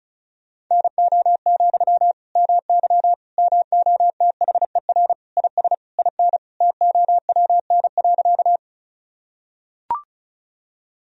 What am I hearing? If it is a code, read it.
Morse code, heard as a keyed tone, about 34 wpm, 700 Hz. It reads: NO, MY MOTHER IS IN TOWN.
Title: Sentences from Top 1000 Words - Review Part 2 - 35wpm